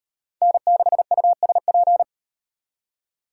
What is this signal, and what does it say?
Morse code, heard as a keyed tone, about 38 wpm, 700 Hz. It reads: N6USP